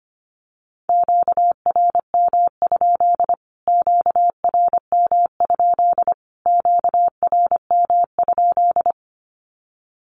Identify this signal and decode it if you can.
Morse code, heard as a keyed tone, about 25 wpm, 700 Hz. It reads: QRM? QRM? QRM?
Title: Q-Codes Repeated 3x 25wpm